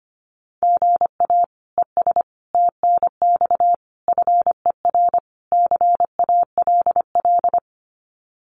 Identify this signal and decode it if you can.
Morse code, heard as a keyed tone, about 25 wpm, 700 Hz. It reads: GA ES TNX FER CALL
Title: Common QSO Phrases Encore 25wpm